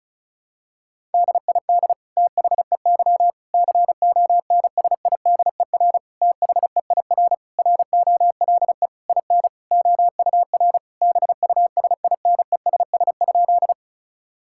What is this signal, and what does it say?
Morse code, heard as a keyed tone, about 36 wpm, 700 Hz. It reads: DID THEY CONSIDER THEIR ROLE IN OUR BUSINESS?
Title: Sentences from Top 1000 Words - Review Part 6 - 35wpm